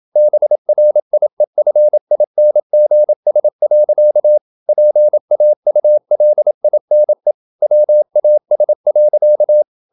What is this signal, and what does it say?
Morse code, heard as a keyed tone, about 27 wpm, 600 Hz. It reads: BRIEFINGS. PAULINE WAS.
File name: Never 0027wpm